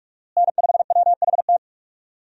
Morse code, heard as a keyed tone, about 46 wpm, 700 Hz. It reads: N5WHT
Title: USA Call Signs - 5 Characters - Repeated 2x - 45wpm